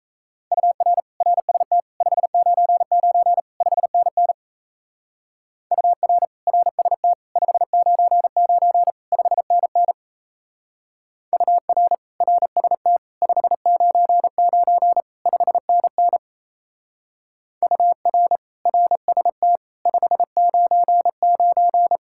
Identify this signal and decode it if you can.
Morse code, heard as a keyed tone, about 33 wpm, 700 Hz. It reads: UR RST 599 5NN UR RST 599 5NN UR RST 599 5NN UR RST 599
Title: Speed-Racing Common QSO Phrases Encore - 28wpm